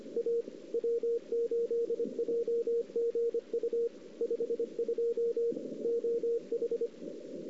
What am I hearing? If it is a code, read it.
Morse code, heard as a keyed tone, about 25 wpm, 470 Hz, with radio noise. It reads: A W8JGU 52 OH
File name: CW / 2008-ss-cw-k5tr-op-n5rz -01:00